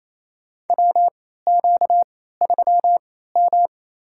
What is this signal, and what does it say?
Morse code, heard as a keyed tone, about 28 wpm, spaced as for 16 wpm, 700 Hz. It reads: WQ3M
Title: USA Call Signs - 4 Characters - 3x Character Spacing - 28wpm